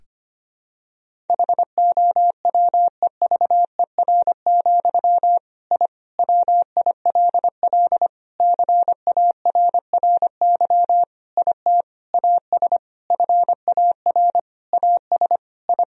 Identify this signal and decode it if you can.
Morse code, heard as a keyed tone, about 25 wpm, 700 Hz. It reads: HOWEVER, I WILL CARRY IT AS FAR AS I
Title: Sentences from Top 1000 Words - Review Part 1 - 25wpm